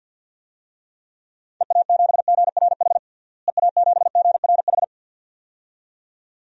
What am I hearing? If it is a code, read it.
Morse code, heard as a keyed tone, about 52 wpm, 700 Hz. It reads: EA7GRH EA7GRH
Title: DXCC-100 Call Signs - 6 Characters - Repeated 2x - 50wpm